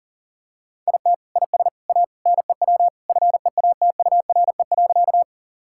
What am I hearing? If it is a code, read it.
Morse code, heard as a keyed tone, about 40 wpm, 700 Hz. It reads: IT IS A NEW FEATURE.
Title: Sentences from Top 1000 Words - Review Part 5 - 40wpm